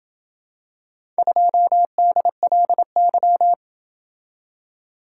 Morse code, heard as a keyed tone, about 27 wpm, 700 Hz. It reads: 2DLY